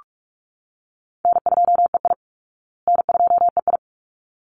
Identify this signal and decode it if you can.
Morse code, heard as a keyed tone, about 44 wpm, 700 Hz. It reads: N2EI N2EI
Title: Call Signs Repeated 3x - 45wpm